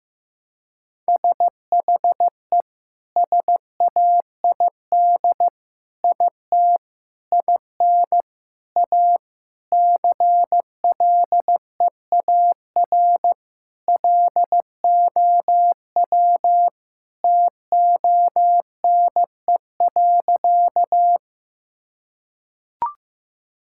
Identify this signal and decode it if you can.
Morse code, heard as a keyed tone, about 15 wpm, 700 Hz. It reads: SHE SAID IT IN A CLEAR LOW TONE.
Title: Sentences from Top 1600 Words 15wpm